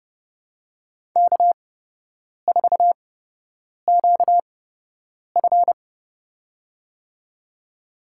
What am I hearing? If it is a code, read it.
Morse code, heard as a keyed tone, about 30 wpm, spaced as for 8 wpm, 700 Hz. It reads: K4QF
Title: USA Call Signs - 8x Character Spacing - 30wpm